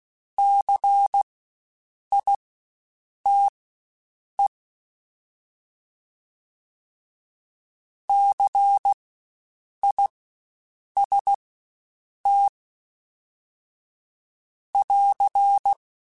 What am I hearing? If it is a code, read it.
Morse code, heard as a keyed tone, about 16 wpm, spaced as for 7 wpm, 800 Hz. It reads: CITE CIST <AR>
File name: Spaced Alphas / Spaced Alphas mp3s / Cs sp @15